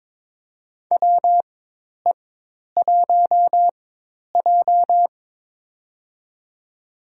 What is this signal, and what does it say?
Morse code, heard as a keyed tone, about 22 wpm, spaced as for 10 wpm, 700 Hz. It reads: WE1J
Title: USA Call Signs - 4 Characters - 4x Character Spacing - 22wpm